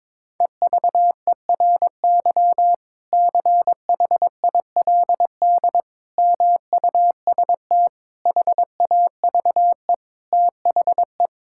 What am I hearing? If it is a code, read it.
Morse code, heard as a keyed tone, about 22 wpm, 700 Hz. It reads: EVERY CHILD MUST HAVE THE
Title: Sentences from Top 1000 Words - Review Part 6 - 22wpm